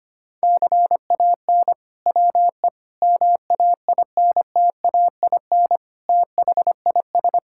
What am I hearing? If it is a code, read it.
Morse code, heard as a keyed tone, about 25 wpm, 700 Hz. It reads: CAN WE MAINTAIN THIS
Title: Sentences from Top 1000 Words - Review Part 5 - 25wpm